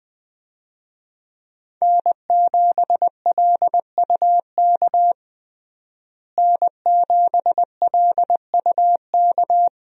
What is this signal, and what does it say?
Morse code, heard as a keyed tone, about 20 wpm, 700 Hz. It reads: N7LUK N7LUK
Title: USA Call Signs - 5 Characters - Repeated 3x - 20wpm